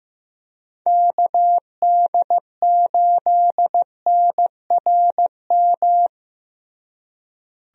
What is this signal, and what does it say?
Morse code, heard as a keyed tone, about 15 wpm, 700 Hz. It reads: KD8NRM
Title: USA Call Signs - 6 Characters - Repeated 2x - 15wpm